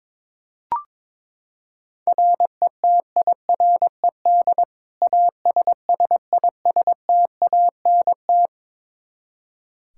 Morse code, heard as a keyed tone, about 22 wpm, 700 Hz. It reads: RETIRED ASSISTANT
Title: Practice QSO - Exchange 3 - More Info - 22wpm